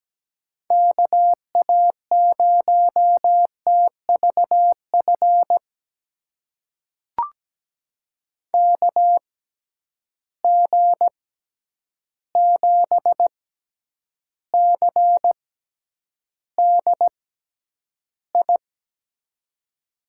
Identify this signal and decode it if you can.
Morse code, heard as a keyed tone, about 17 wpm, spaced as for 9 wpm, 700 Hz. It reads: KA0TVF KG7CDI
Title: USA Call Signs - 6 Characters - 6x Character Spacing - 17wpm